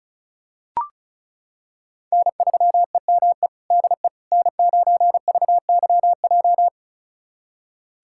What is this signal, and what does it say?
Morse code, heard as a keyed tone, about 34 wpm, 700 Hz. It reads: N3EME DE N9VYJ
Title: USA Callsign DE Callsign - 5 Characters - 35wpm